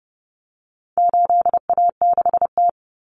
Morse code, heard as a keyed tone, about 30 wpm, 700 Hz. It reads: 8A6T
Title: Common Cut Numbers - 4-Digit 30wpm